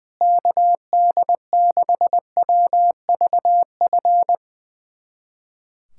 Morse code, heard as a keyed tone, about 20 wpm, 700 Hz. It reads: KD6WVF
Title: USA Callsign DE Callsign - 6 Characters - 20wpm